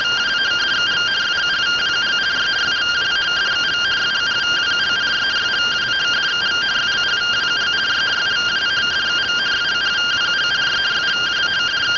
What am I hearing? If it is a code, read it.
Two-tone teleprinter signal, about 170 Hz shift.
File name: findings / MR CORONAL 55.gen-20 rtty rx